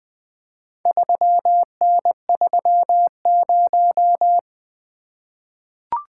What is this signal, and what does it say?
Morse code, heard as a keyed tone, about 20 wpm, 700 Hz. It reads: 3N30
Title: Four Letter-Number Combinations - 20wpm